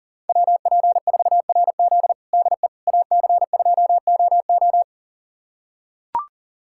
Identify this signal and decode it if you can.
Morse code, heard as a keyed tone, about 37 wpm, 700 Hz. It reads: WP4RZ DE AC2OO
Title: USA Callsign DE Callsign - 5 Characters - 40wpm